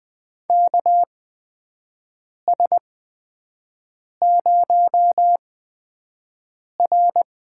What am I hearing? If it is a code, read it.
Morse code, heard as a keyed tone, about 20 wpm, spaced as for 5 wpm, 700 Hz. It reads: KS0R
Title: USA Call Signs - 4 Characters - 8x Character Spacing - 20wpm